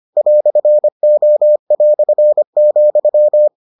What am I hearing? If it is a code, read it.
Morse code, heard as a keyed tone, about 25 wpm, 600 Hz. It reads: "O",